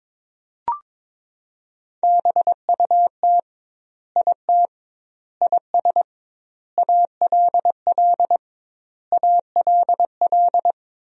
Morse code, heard as a keyed tone, about 22 wpm, 700 Hz. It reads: BUT IT IS ALL ALL
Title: 2x Word-Spacing - Sentences from Top 300 Words - 22wpm